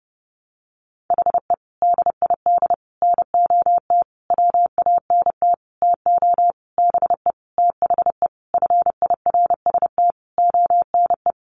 Morse code, heard as a keyed tone, about 30 wpm, 700 Hz. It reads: HE DID NOT WANT TO BE THE FIRST ONE
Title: Sentences from Top 1300 Words 30wpm